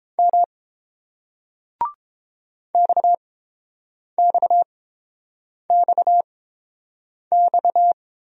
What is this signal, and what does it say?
Morse code, heard as a keyed tone, about 29 wpm, spaced as for 7 wpm, 690 Hz. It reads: M XXXX